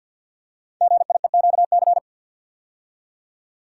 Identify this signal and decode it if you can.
Morse code, heard as a keyed tone, about 47 wpm, 700 Hz. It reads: GIEQC
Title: Code Groups - 5 Letter:Numbers - 50wpm